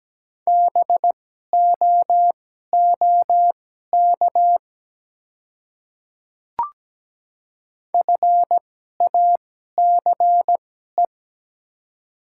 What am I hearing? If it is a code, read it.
Morse code, heard as a keyed tone, about 17 wpm, spaced as for 12 wpm, 700 Hz. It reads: BOOK FACE
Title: 2x Character Spacing - Top 300 Words - 17wpm